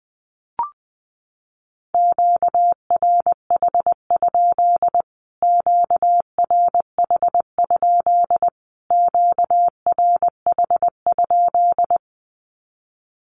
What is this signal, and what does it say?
Morse code, heard as a keyed tone, about 19 wpm, 700 Hz. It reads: QRH? QRH? QRH?
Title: Q-Codes Repeated 3x 20wpm